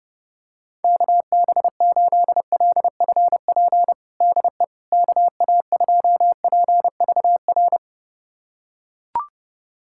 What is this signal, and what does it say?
Morse code, heard as a keyed tone, about 30 wpm, 700 Hz. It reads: KB8LFP DE KA2PVR
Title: USA Callsign DE Callsign - 6 Characters - 30wpm